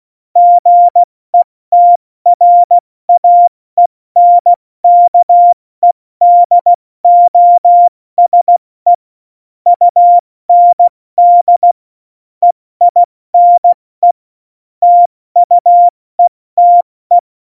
Morse code, heard as a keyed tone, about 16 wpm, 700 Hz. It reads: GETRAENKEDOSE UND EINE TUETE
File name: Morsekurs / Witze / Witz 12. Februar 2019, 70 F80